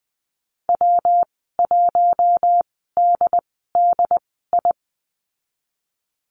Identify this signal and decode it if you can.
Morse code, heard as a keyed tone, about 20 wpm, spaced as for 15 wpm, 700 Hz. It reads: W1DDI
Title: 179 Lesson 29g - Slash - Call Signs 20wpm